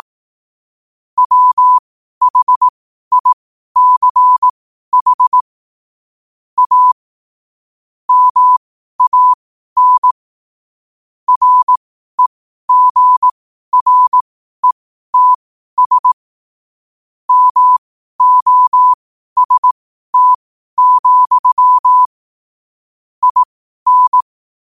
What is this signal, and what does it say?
Morse code, heard as a keyed tone, about 18 wpm, spaced as for 12 wpm, 1000 Hz. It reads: WHICH A MAN REGRETS MOST, IN
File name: qotdfiles / 12wpm